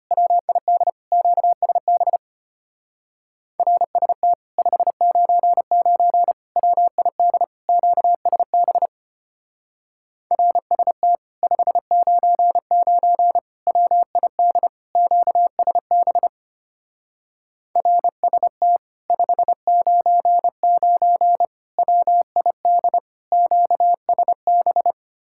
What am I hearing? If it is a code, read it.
Morse code, heard as a keyed tone, about 29 wpm, 700 Hz. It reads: WID QSB RST 599 WID QSB RST 599 WID QSB RST 599 WID QSB
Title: Speed-Racing Common QSO Phrases - 25wpm